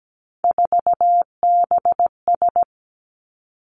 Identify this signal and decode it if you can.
Morse code, heard as a keyed tone, about 17 wpm, 700 Hz. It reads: 4BS
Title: Three Challenging Letter-Number Combinations 17wpm